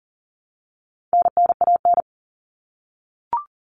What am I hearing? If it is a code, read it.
Morse code, heard as a keyed tone, about 40 wpm, 700 Hz. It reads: NNAN
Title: All Cut Numbers - 4-Digit 40wpm